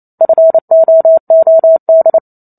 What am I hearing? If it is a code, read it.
Morse code, heard as a keyed tone, about 28 wpm, 650 Hz. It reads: FOOD